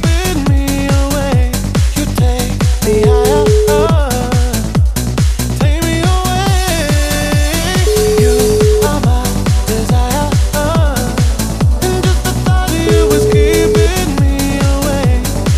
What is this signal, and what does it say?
Ringback tone (Europe, 425 Hz), with noise.